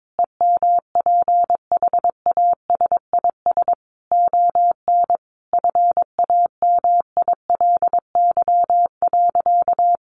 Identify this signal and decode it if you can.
Morse code, heard as a keyed tone, about 22 wpm, 700 Hz. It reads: EMPHASIS ON FAMILY.
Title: Sentences from Top 1400 Words 22wpm